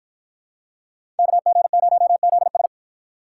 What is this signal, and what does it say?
Morse code, heard as a keyed tone, about 52 wpm, 700 Hz. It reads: KG0ZS